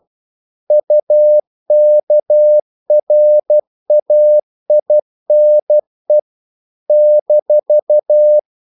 Morse code, heard as a keyed tone, about 12 wpm, 600 Hz. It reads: UKRAINE -
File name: CW / Practice / NationalCapitals12